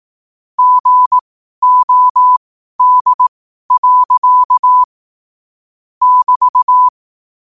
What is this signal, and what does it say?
Morse code, heard as a keyed tone, about 18 wpm, spaced as for 12 wpm, 1000 Hz. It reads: GOD. <BT>
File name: qotdfiles / 12wpm